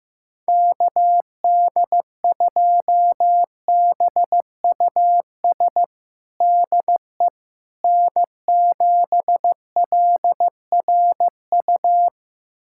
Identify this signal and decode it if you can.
Morse code, heard as a keyed tone, about 16 wpm, 700 Hz. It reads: KD2BUS DE N7LRU
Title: USA Callsign DE Callsign 15wpm